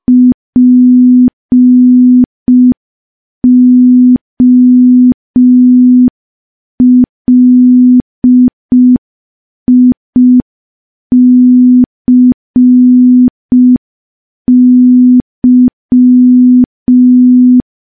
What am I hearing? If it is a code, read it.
Morse code, heard as a keyed tone, about 5 wpm, 250 Hz. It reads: POLICY
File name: morse / morse-training-05WPM